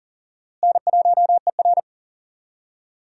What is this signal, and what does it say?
Morse code, heard as a keyed tone, about 40 wpm, 700 Hz. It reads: N1ER
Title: USA Call Signs - 4 Characters - Repeated 3x - 40wpm